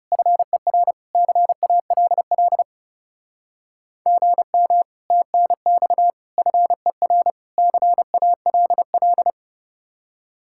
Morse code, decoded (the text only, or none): FER CALL GM TNX FER CALL